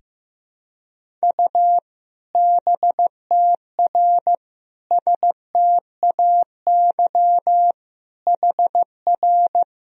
Morse code, heard as a keyed tone, about 15 wpm, 700 Hz. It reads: U BTR STAY HR
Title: Sentences from Top 500 Words and Abbreviations 15wpm